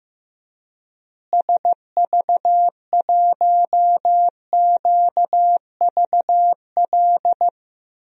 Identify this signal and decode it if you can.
Morse code, heard as a keyed tone, about 15 wpm, 700 Hz. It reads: SV1QVL
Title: DXCC-100 Call Signs - 6 Characters - Repeated 2x - 15wpm